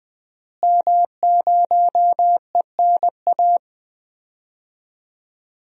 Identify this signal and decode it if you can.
Morse code, heard as a keyed tone, about 20 wpm, 700 Hz. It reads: M0ENA
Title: DXCC-100 Call Signs - 5 Characters - Repeated 3x - 20wpm